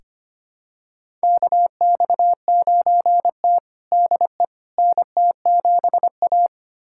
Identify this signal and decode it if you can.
Morse code, heard as a keyed tone, about 25 wpm, 700 Hz. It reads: KX9T DE NT7A
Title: USA Callsign DE Callsign - 4 Characters - 25wpm